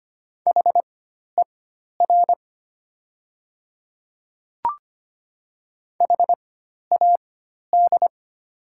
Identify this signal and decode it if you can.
Morse code, heard as a keyed tone, about 25 wpm, spaced as for 12 wpm, 700 Hz. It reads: HER HAD
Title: 4x Character Spacing - Top 100 Words - 25wpm